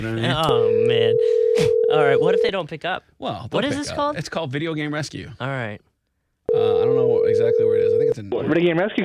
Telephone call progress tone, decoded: ringback tone